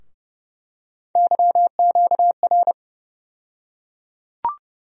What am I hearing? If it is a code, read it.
Morse code, heard as a keyed tone, about 30 wpm, 700 Hz. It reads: YQR
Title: USA Call Sign Postfixes - 3 Letters 30wpm